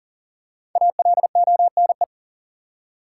Morse code, heard as a keyed tone, about 40 wpm, 700 Hz. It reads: ALONE
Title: Five Letter Words - 40wpm